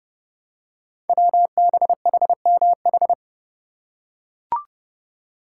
Morse code, heard as a keyed tone, about 30 wpm, 700 Hz. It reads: WBHMH
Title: Code Groups - 5 Letter:Numbers - 30wpm